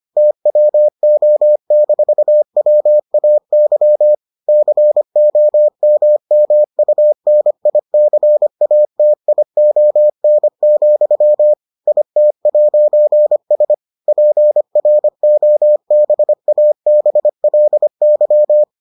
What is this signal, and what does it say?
Morse code, heard as a keyed tone, about 25 wpm, 600 Hz. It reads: TWO-WAY COMMUNICATION, IT'S PROBABLY